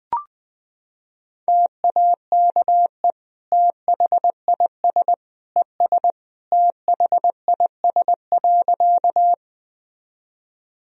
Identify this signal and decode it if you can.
Morse code, heard as a keyed tone, about 21 wpm, 700 Hz. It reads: TAKE THIS ES THIS.
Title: Sentences from Top 200 Words and Abbreviations 20wpm